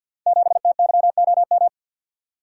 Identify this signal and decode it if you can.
Morse code, heard as a keyed tone, about 49 wpm, 700 Hz. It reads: 7TYOM
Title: Code Groups - 5 Letter:Numbers - 50wpm